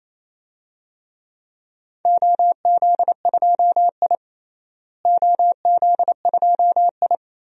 Morse code, heard as a keyed tone, about 28 wpm, 700 Hz. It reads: OZ2I OZ2I
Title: DXCC-100 Call Signs - 4 Characters - Repeated 2x - 28wpm